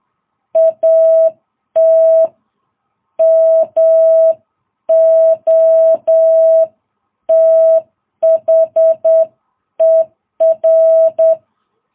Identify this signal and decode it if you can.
Morse code, heard as a keyed tone, about 9 wpm, 650 Hz. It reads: AT MOTHER
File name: jh8cbh